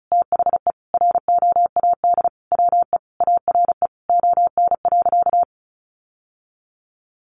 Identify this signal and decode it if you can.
Morse code, heard as a keyed tone, about 34 wpm, 700 Hz. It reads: THE ROAD WE ARE ON.
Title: Sentences from Top 1100 Words 35wpm